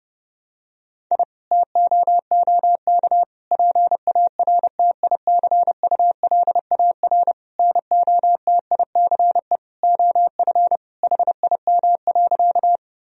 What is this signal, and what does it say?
Morse code, heard as a keyed tone, about 32 wpm, 700 Hz. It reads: I TOOK PARTICULAR NOTICE OF HIM.
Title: Sentences from Top 1000 Words - Review Part 4 - 30wpm